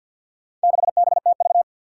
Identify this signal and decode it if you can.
Morse code, heard as a keyed tone, about 53 wpm, 700 Hz. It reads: 6BTV
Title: Four Challenging Letter-Number Combinations 50wpm